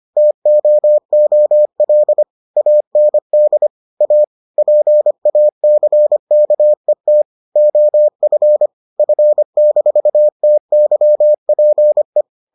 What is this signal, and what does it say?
Morse code, heard as a keyed tone, about 25 wpm, 600 Hz. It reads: TOOL AND A PACKET OF F-TYPE